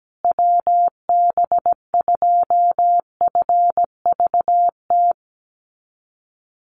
Morse code, heard as a keyed tone, about 17 wpm, 700 Hz. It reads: WB2FVT
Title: USA Call Signs Repeated 3x 17wpm